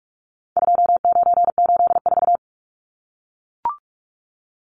Morse code, heard as a keyed tone, about 44 wpm, 700 Hz. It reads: 2984
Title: Numbers - 4-Digit 45wpm